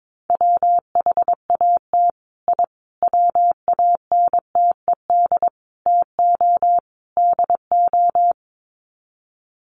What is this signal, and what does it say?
Morse code, heard as a keyed tone, about 22 wpm, 700 Hz. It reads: WHAT I WANTED TO DO
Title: Sets of 5 Words Encore2 22wpm